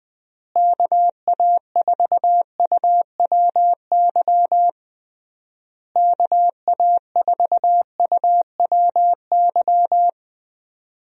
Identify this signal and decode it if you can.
Morse code, heard as a keyed tone, about 21 wpm, 700 Hz. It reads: KA4UWY KA4UWY
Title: USA Call Signs - 6 Characters - Repeated 3x - 20wpm